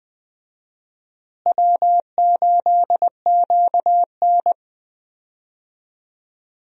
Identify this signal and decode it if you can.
Morse code, heard as a keyed tone, about 21 wpm, 700 Hz. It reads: W8QN
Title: USA Call Signs - 4 Characters - Repeated 3x - 20wpm